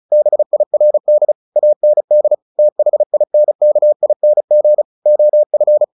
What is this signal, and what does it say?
Morse code, heard as a keyed tone, about 34 wpm, 600 Hz. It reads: BIRD AND THINKING OF
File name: the old man and the sea / the old man and the sea 1 35wpm